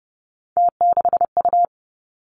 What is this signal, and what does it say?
Morse code, heard as a keyed tone, about 30 wpm, 700 Hz. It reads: T6U
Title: Three Challenging Letter-Number Combinations 30wpm